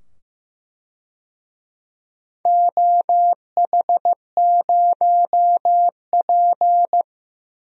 Morse code, heard as a keyed tone, about 15 wpm, 700 Hz. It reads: OH0P
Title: DXCC-100 Call Signs - 4 Characters - 15wpm